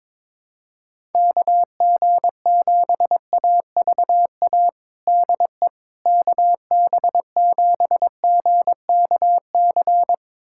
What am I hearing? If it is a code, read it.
Morse code, heard as a keyed tone, about 21 wpm, 700 Hz. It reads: KG7AVA DE KB7GKC
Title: USA Callsign DE Callsign - 6 Characters - 22wpm